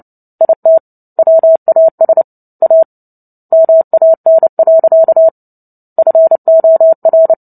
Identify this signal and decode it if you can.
Morse code, heard as a keyed tone, about 30 wpm, 650 Hz. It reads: IT WAS A MAN. FOR
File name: Book Two / 03-Chapter 3 - 29 WPM